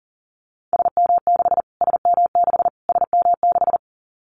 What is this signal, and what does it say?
Morse code, heard as a keyed tone, about 40 wpm, 700 Hz. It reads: SM6 SM6 SM6
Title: Callsign Prefixes for Top 25 DXCC - Repeated 3x - 40wpm